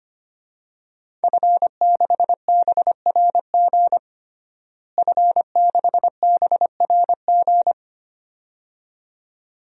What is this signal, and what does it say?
Morse code, heard as a keyed tone, about 25 wpm, 700 Hz. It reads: F6BRG F6BRG
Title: DXCC-100 Call Signs - 5 Characters - Repeated 2x - 25wpm